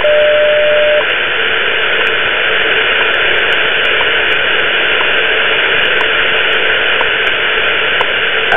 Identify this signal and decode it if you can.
Radio time signal, with noise.
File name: WWV time signal